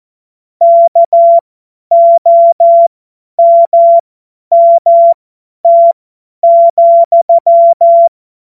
Morse code, heard as a keyed tone, about 14 wpm, spaced as for 10 wpm, 680 Hz. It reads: KOMMT,
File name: Morsekurs / Witze / Witz 02 April 2024 50 F70